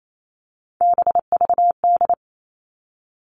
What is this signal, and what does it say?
Morse code, heard as a keyed tone, about 28 wpm, 700 Hz. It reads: BVD
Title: Three Challenging Letter-Number Combinations 28wpm